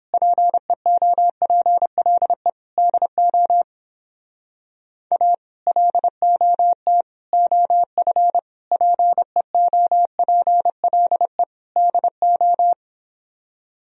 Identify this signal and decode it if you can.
Morse code, heard as a keyed tone, about 29 wpm, 700 Hz. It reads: PEOPLE DO A LOT OF PEOPLE DO